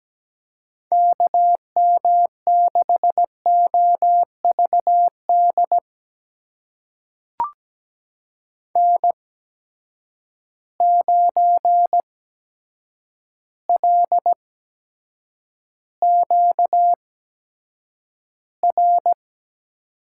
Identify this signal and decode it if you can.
Morse code, heard as a keyed tone, about 17 wpm, spaced as for 8 wpm, 700 Hz. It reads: KM6OVD N9LQR
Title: USA Call Signs - 8x Character Spacing - 17wpm